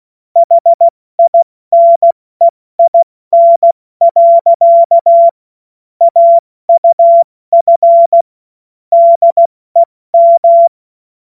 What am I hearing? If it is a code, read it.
Morse code, heard as a keyed tone, about 16 wpm, 680 Hz. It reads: HINEIN. AUF DEM
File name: Morsekurs / Witze / Witz 06. September 2022 70 F80